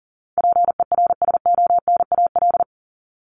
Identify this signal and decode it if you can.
Morse code, heard as a keyed tone, about 40 wpm, 700 Hz. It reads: PERSONAL
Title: Top 501-750 Words 40wpm